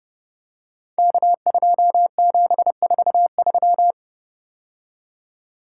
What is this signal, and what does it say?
Morse code, heard as a keyed tone, about 30 wpm, 700 Hz. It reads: K2743
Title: Code Groups - 5 Letter:Numbers - 30wpm